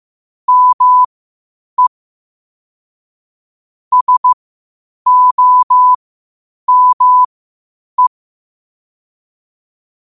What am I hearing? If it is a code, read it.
Morse code, heard as a keyed tone, about 15 wpm, spaced as for 8 wpm, 1000 Hz. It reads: ME SOME
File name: qotdfiles / 8wpm